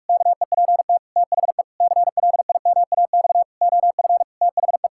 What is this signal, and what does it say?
Morse code, heard as a keyed tone, about 46 wpm, 700 Hz. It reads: KEPT THE CLIMAX OF THE
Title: Pangram Challenge v2 - 45wpm